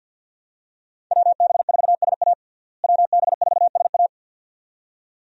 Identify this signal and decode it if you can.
Morse code, heard as a keyed tone, about 47 wpm, 700 Hz. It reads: WB4SA WB4SA
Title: USA Call Signs - 5 Characters - Repeated 2x - 50wpm